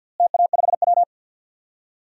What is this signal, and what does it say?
Morse code, heard as a keyed tone, about 48 wpm, 700 Hz. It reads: TA5W
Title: DXCC-100 Call Signs - 4 Characters - Repeated 2x - 50wpm